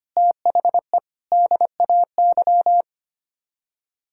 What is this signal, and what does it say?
Morse code, heard as a keyed tone, about 25 wpm, 700 Hz. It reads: THE DAY
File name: Sets of 9 Words - 25wpm